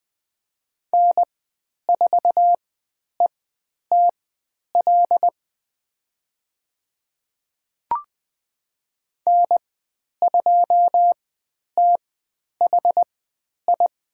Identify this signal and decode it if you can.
Morse code, heard as a keyed tone, about 20 wpm, spaced as for 10 wpm, 700 Hz. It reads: N4ETL N2THI
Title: 109 Lesson 18g - Letter m - Call Signs 20wpm